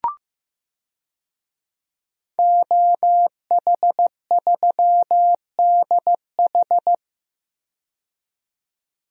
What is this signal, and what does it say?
Morse code, heard as a keyed tone, about 15 wpm, 700 Hz. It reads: OH3DH
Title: DXCC-100 Call Signs - 5 Characters - 15wpm